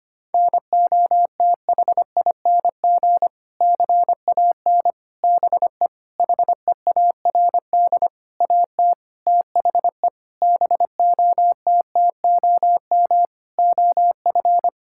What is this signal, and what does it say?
Morse code, heard as a keyed tone, about 25 wpm, 700 Hz. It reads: NOTHING CAN BE HEARD AT THE BOTTOM OF
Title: Sentences from Top 1000 Words - Review Part 6 - 25wpm